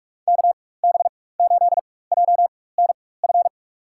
Morse code, heard as a keyed tone, about 46 wpm, spaced as for 21 wpm, 700 Hz. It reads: KB8JNF
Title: USA Call Signs - 6 Characters - 4x Character Spacing - 45wpm